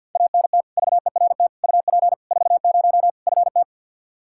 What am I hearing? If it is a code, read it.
Morse code, heard as a keyed tone, about 52 wpm, 700 Hz. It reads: ANT VERT UP 40 FT
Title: Practice QSO - Exchange 2 - Info - 50wpm